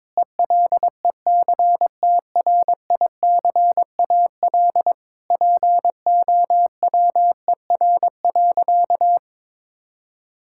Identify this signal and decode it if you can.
Morse code, heard as a keyed tone, about 22 wpm, 700 Hz. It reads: ELECTRICAL POWER.
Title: Sentences from Top 1000 Words - Review Part 6 - 22wpm